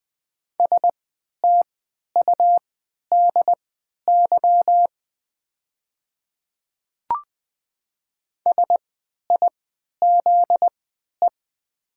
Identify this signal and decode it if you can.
Morse code, heard as a keyed tone, about 20 wpm, spaced as for 11 wpm, 700 Hz. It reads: STUDY SIZE
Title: 3x Character Spacing - Top 500 Words - 20wpm